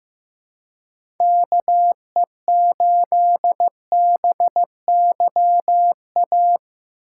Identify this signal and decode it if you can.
Morse code, heard as a keyed tone, about 15 wpm, 700 Hz. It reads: KE8BYA
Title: USA Call Signs - 6 Characters - 3x Character Spacing - 15wpm